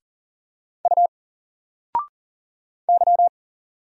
Morse code, heard as a keyed tone, about 40 wpm, 700 Hz. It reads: U Y
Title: Single Letter-Number 40wpm